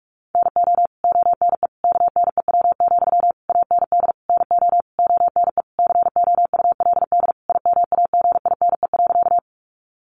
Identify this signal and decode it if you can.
Morse code, heard as a keyed tone, about 44 wpm, 700 Hz. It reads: NO ONE KNEW, AND NO ONE COULD IMAGINE.
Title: Sentences from Top 1400 Words 45wpm